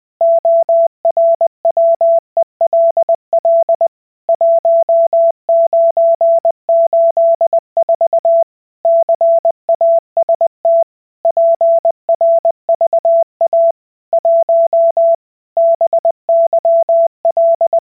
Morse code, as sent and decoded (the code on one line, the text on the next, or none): --- .-. .-- . .-.. .-.. / .---- ----. ---.. ....- / -.-. .- ... - / .--. .-. ...- .- / .---- / -... -.-- .-..
ORWELL 1984 CAST PRVA 1 BYL